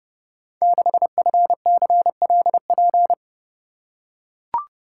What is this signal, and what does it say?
Morse code, heard as a keyed tone, about 30 wpm, 700 Hz. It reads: 6FCLP
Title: Code Groups - 5 Letter:Numbers - 30wpm